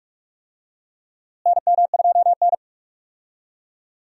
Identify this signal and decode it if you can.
Morse code, heard as a keyed tone, about 46 wpm, 700 Hz. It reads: NM2N
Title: USA Call Signs - 4 Characters - Repeated 3x - 45wpm